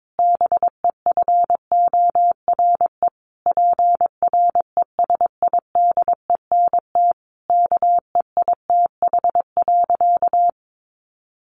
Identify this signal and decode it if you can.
Morse code, heard as a keyed tone, about 22 wpm, 700 Hz. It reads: BEFORE PRESIDENT KEITH.
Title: A Sound Of Thunder 22wpm